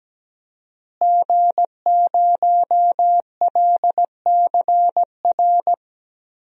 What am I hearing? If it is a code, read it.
Morse code, heard as a keyed tone, about 17 wpm, 700 Hz. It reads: G0LCR
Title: DXCC-100 Call Signs - 5 Characters - Repeated 2x - 17wpm